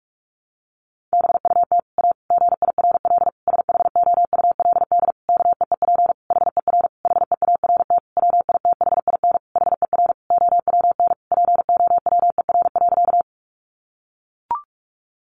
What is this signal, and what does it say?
Morse code, heard as a keyed tone, about 44 wpm, 700 Hz. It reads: BUT A GIRL SHOULD KEEP HER HEART WITHIN HER OWN POWER.
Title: Sentences from Top 1200 Words 45wpm